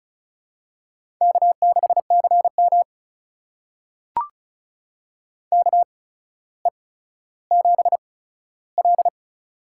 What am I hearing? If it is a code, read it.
Morse code, heard as a keyed tone, about 36 wpm, spaced as for 15 wpm, 700 Hz. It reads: K6CM KE7L
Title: USA Call Signs - 4 Characters - 8x Character Spacing - 35wpm